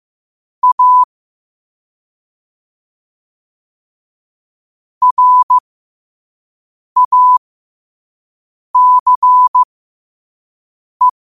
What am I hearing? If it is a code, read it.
Morse code, heard as a keyed tone, about 15 wpm, spaced as for 5 wpm, 1000 Hz. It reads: A RACE
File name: qotdfiles / 5wpm